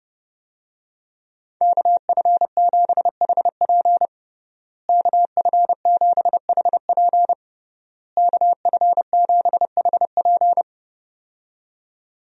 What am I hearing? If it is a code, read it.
Morse code, heard as a keyed tone, about 32 wpm, 700 Hz. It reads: KF7HP KF7HP KF7HP
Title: USA Call Signs - 5 Characters - Repeated 3x - 30wpm